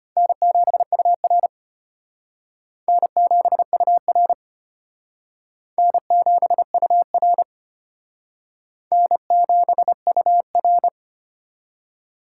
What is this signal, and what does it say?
Morse code, heard as a keyed tone, about 33 wpm, 700 Hz. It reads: N7UR N7UR N7UR N7UR